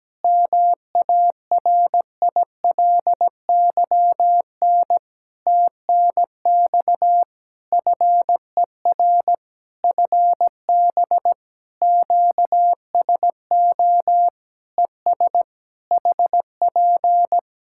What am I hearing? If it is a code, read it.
Morse code, heard as a keyed tone, about 17 wpm, 700 Hz. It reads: MARILYN TNX FER FB QSO ES HP
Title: Practice QSO - Exchange 4 - Ending - 17wpm